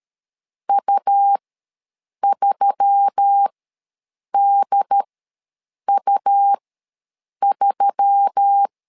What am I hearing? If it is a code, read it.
Morse code, heard as a keyed tone, about 13 wpm, spaced as for 7 wpm, 800 Hz. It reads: U3DU3
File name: LEZIONI / 4 Q1S3 ws 12wpm 2038